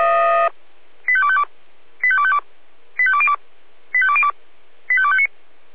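Five-tone selective call (ZVEI), heard as quick runs of five tones, tones of about 70 ms. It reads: 85252 85252 85202 85202 85259
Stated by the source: ZVEI five-tone call 85855, received by SDR.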